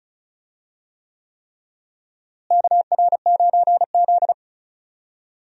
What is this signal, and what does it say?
Morse code, heard as a keyed tone, about 36 wpm, 700 Hz. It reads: KR9Z
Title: USA Call Signs - 4 Characters - Repeated 3x - 35wpm